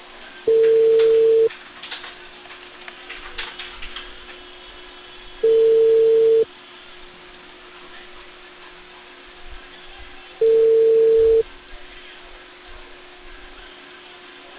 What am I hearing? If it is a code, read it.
Ringback tone (China, 450 Hz).